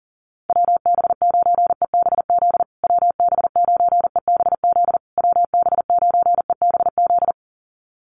Morse code, heard as a keyed tone, about 40 wpm, 700 Hz. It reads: WB9EBZ WB9EBZ WB9EBZ
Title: USA Call Signs Repeated 3x 40wpm